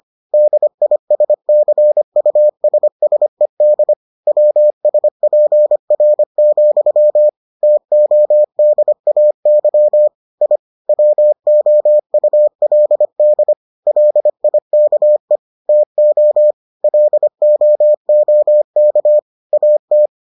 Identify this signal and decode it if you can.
Morse code, heard as a keyed tone, about 25 wpm, 600 Hz. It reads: DISCUSSED WSPR, TODAY I WOULD LIKE TO LOOK AT